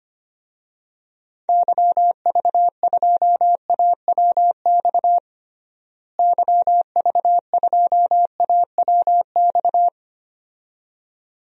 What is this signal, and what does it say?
Morse code, heard as a keyed tone, about 25 wpm, 700 Hz. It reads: YV2AWX YV2AWX
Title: DXCC-100 Call Signs - 6 Characters - Repeated 2x - 25wpm